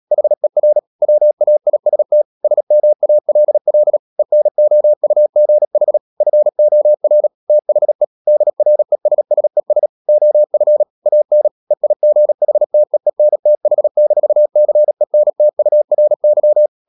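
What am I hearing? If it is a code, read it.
Morse code, heard as a keyed tone, about 37 wpm, 600 Hz. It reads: HER WAIST SMALL ENOUGH FOR THE DRESSES OF AN EIGHTEENTH-CENTURY
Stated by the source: Morse code practice at 35–39 wpm sending English text of Ken Follett's novel Never, from CW med Gnister.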